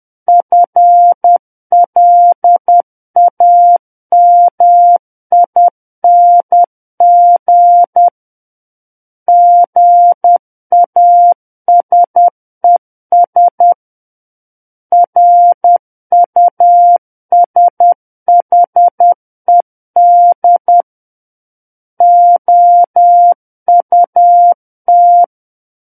Morse code, heard as a keyed tone, about 10 wpm, 700 Hz. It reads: FLAMING GASES RUSHED OUT
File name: Book One / 01-Chapter 1 - 10 WPM